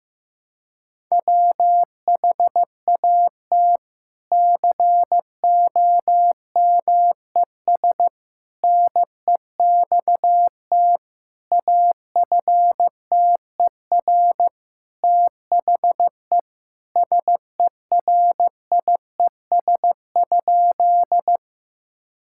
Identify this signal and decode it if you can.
Morse code, heard as a keyed tone, about 15 wpm, 700 Hz. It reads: WHAT COMES NEXT AFTER THE SERIES?
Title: Sentences from Top 1000 Words - Review Part 5 - 15wpm